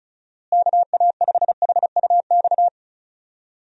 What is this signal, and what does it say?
Morse code, heard as a keyed tone, about 36 wpm, 700 Hz. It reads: KA5HUX
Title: USA Call Signs - 6 Characters - Repeated 2x - 35wpm